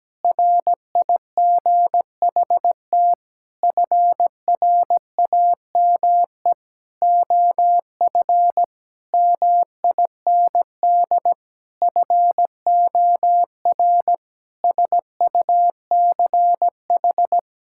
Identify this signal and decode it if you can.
Morse code, heard as a keyed tone, about 17 wpm, 700 Hz. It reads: RIGHT FRAME OF MIND FOR SUCH
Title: Sentences from Top 1000 Words - Review Part 6 - 17wpm